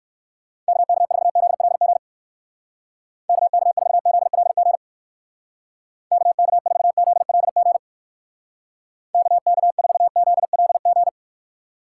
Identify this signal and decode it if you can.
Morse code, heard as a keyed tone, about 56 wpm, 700 Hz. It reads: KK4ZLG KK4ZLG KK4ZLG KK4ZLG